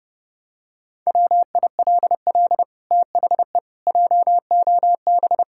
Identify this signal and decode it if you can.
Morse code, heard as a keyed tone, about 30 wpm, 700 Hz. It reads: WILL THE JOB
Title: Sentences from Top 1600 Words 30wpm